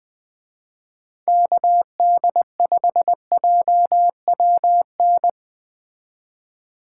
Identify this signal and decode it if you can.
Morse code, heard as a keyed tone, about 21 wpm, 700 Hz. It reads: KD5JWN